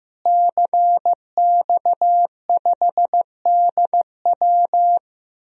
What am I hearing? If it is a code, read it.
Morse code, heard as a keyed tone, about 16 wpm, 700 Hz. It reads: CX5DW
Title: DXCC-100 Call Signs - 5 Characters - Repeated 3x - 15wpm